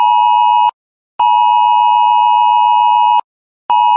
Emergency Alert System (SAME) tones (the 853 and 960 Hz attention tone). It is heard from 1.2 to 3.3 s.
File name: cloud download